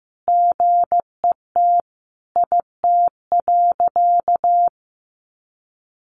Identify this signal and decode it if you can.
Morse code, heard as a keyed tone, about 15 wpm, 700 Hz. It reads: GET IT.
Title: Sentences from Top 1500 Words 15wpm